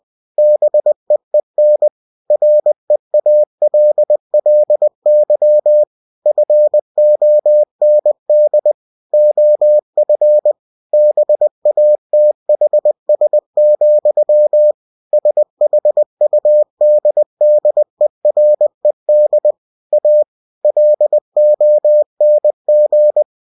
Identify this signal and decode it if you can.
Morse code, heard as a keyed tone, about 20 wpm, 600 Hz. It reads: BEEN REALLY FOND OF BATHS, SHUDDERED A LONG
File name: cw-books / winnie-the-pooh / winnie-20wpm-page-0071